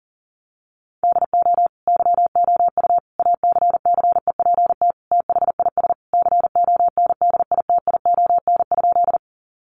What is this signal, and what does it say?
Morse code, heard as a keyed tone, about 40 wpm, 700 Hz. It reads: DO YOU ACCEPT THIS CONDITION?
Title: Sentences from Top 1300 Words 40wpm